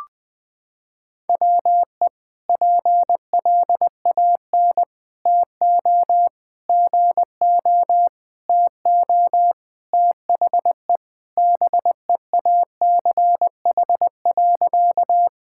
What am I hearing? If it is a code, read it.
Morse code, heard as a keyed tone, about 21 wpm, 700 Hz. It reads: WE PLAN TO GO TO THE BEACH.
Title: Sentences with 4-Word Phrases - 20wpm